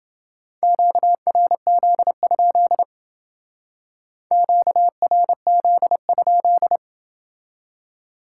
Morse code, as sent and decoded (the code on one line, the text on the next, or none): --.- .-. --.. ..--.. / --.- .-. --.. ..--..
QRZ? QRZ?